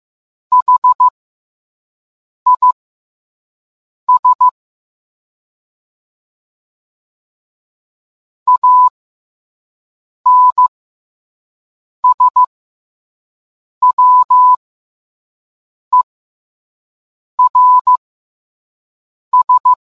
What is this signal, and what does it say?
Morse code, heard as a keyed tone, about 15 wpm, spaced as for 5 wpm, 1000 Hz. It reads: HIS ANSWERS